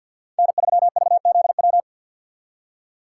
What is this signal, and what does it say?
Morse code, heard as a keyed tone, about 51 wpm, 700 Hz. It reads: N3VZW
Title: USA Call Signs - 5 Characters - 8x Character Spacing - 50wpm